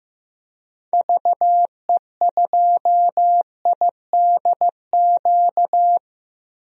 Morse code, heard as a keyed tone, about 15 wpm, 700 Hz. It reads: VE2IDQ
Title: DXCC-100 Call Signs - 6 Characters - Repeated 2x - 15wpm